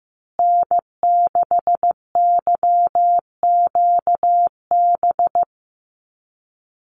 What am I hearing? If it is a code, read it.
Morse code, heard as a keyed tone, about 15 wpm, 700 Hz. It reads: N6YQB
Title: USA Call Signs Repeated 2x 15wpm